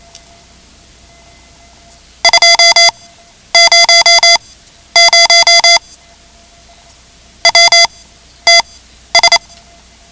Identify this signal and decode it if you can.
Morse code, heard as a keyed tone, about 28 wpm, spaced as for 12 wpm, 2050 Hz. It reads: 200 WTS
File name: ham radio / CodeQuick / Code Quick 10